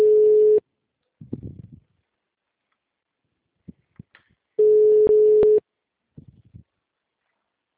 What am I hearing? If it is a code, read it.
Ringback tone (Europe, 425 Hz).
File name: out- -2005- - - .133